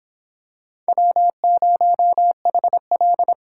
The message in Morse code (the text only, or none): W0HL